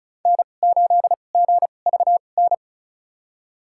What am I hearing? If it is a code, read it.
Morse code, heard as a keyed tone, about 35 wpm, spaced as for 25 wpm, 690 Hz. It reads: N8GVN
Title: USA Call Signs - 5 Characters - 2x Character Spacing - 35wpm